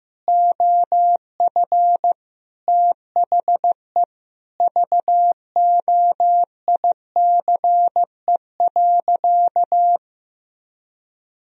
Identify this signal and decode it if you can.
Morse code, heard as a keyed tone, about 15 wpm, 700 Hz. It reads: OF THE VOICE.
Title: Sentences from Top 1000 Words - Review Part 2 - 15wpm